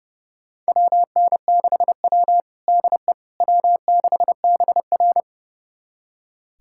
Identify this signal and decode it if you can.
Morse code, heard as a keyed tone, about 32 wpm, 700 Hz. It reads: WN6W DE W6BR
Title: USA Callsign DE Callsign - 4 Characters - 30wpm